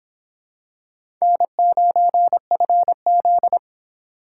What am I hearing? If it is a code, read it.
Morse code, heard as a keyed tone, about 24 wpm, 700 Hz. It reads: N9FZ